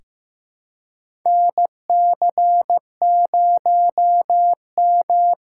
Morse code, heard as a keyed tone, about 15 wpm, 700 Hz. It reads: NC0M
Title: USA Call Signs - 4 Characters - 2x Character Spacing - 15wpm